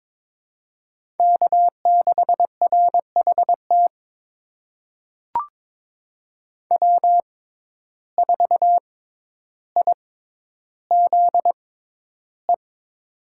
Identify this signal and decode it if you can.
Morse code, heard as a keyed tone, about 22 wpm, spaced as for 11 wpm, 700 Hz. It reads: K6RHT W4IZE
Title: USA Call Signs - 5 Characters - 6x Character Spacing - 22wpm